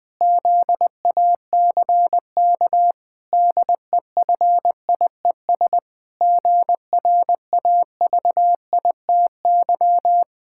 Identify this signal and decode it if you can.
Morse code, heard as a keyed tone, about 20 wpm, 700 Hz. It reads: ZACK DEFIES GRAVITY
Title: Pangram Challenge v2 - 20wpm